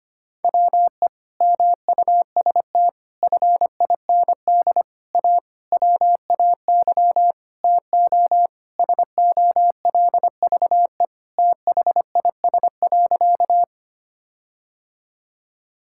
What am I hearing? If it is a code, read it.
Morse code, heard as a keyed tone, about 25 wpm, 700 Hz. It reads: WE MUST FIND A WAY TO SOLVE THIS.
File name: practice / 25wpm / Sentences with 4-Word Phrases - 25wpm